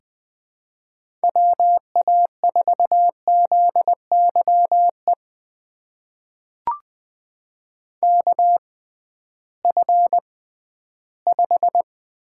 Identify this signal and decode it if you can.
Morse code, heard as a keyed tone, about 20 wpm, spaced as for 13 wpm, 700 Hz. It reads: WA4ZYE KF5
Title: USA Call Signs - 6 Characters - 6x Character Spacing - 20wpm